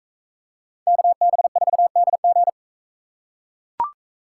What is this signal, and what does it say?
Morse code, heard as a keyed tone, about 42 wpm, 700 Hz. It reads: KB4DG